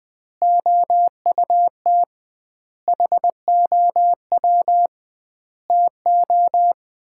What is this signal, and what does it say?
Morse code, heard as a keyed tone, about 20 wpm, 700 Hz. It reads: OUT HOW TO
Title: 2x Word-Spacing - Sets of 5 Words - 20wpm